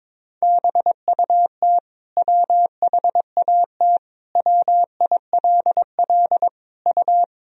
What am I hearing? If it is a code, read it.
Morse code, heard as a keyed tone, about 22 wpm, 700 Hz. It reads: BUT WHAT WILL U
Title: Sentences from Top 100 Words and Abbreviations 22wpm